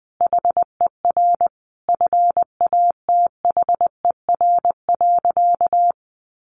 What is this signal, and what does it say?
Morse code, heard as a keyed tone, about 20 wpm, 700 Hz. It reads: HER FATHER.
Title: Sentences from Top 1500 Words 20wpm